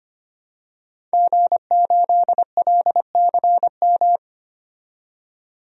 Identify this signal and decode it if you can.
Morse code, heard as a keyed tone, about 25 wpm, 700 Hz. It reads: G8LCM
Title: DXCC-100 Call Signs - 5 Characters - 25wpm